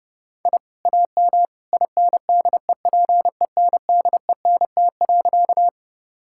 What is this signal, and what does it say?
Morse code, heard as a keyed tone, about 30 wpm, 700 Hz. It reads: I AM INDEPENDENT.
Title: Sentences from Top 1000 Words - Review Part 6 - 30wpm